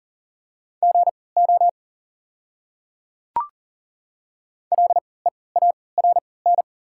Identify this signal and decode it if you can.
Morse code, heard as a keyed tone, about 40 wpm, spaced as for 23 wpm, 700 Hz. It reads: GO LEARN
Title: 3x Character Spacing - Top 300 Words - 40wpm